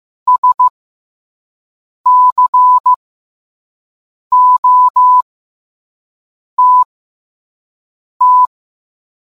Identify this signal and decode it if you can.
Morse code, heard as a keyed tone, about 15 wpm, spaced as for 5 wpm, 1000 Hz. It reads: SCOTT